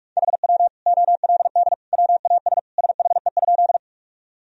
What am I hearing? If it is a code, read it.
Morse code, heard as a keyed tone, about 43 wpm, 700 Hz. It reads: HW OLD WAS SHE?
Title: Sentences from Top 200 Words and Abbreviations 45wpm